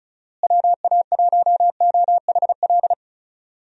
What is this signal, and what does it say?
Morse code, heard as a keyed tone, about 34 wpm, 700 Hz. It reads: WA1OHL
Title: USA Call Signs - 6 Characters - Repeated 2x - 35wpm